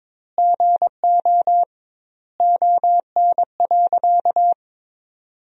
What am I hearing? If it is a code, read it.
Morse code, heard as a keyed tone, about 22 wpm, 700 Hz. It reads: GO ON.
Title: 2x Word-Spacing - Sentences from Top 500 Words - 22wpm